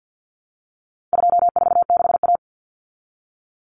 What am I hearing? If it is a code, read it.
Morse code, heard as a keyed tone, about 48 wpm, 700 Hz. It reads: 246A
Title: Common Cut Numbers - 4-Digit 50wpm